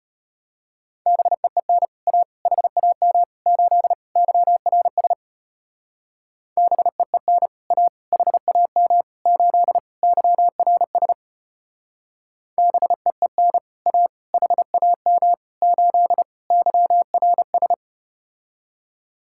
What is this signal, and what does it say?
Morse code, heard as a keyed tone, about 33 wpm, 700 Hz. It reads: BEEN A HAM 8 YRS BEEN A HAM 8 YRS BEEN A HAM 8 YRS